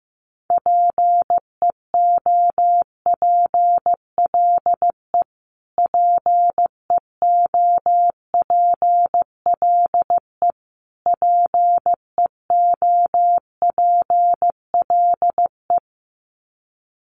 Morse code, heard as a keyed tone, about 15 wpm, 700 Hz. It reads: PEOPLE PEOPLE PEOPLE
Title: Top 200 Words Repeated 15wpm